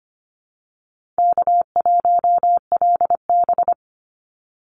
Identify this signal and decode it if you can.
Morse code, heard as a keyed tone, about 25 wpm, 700 Hz. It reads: K1LB
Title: Easy Call Signs Repeated 3x - 25wpm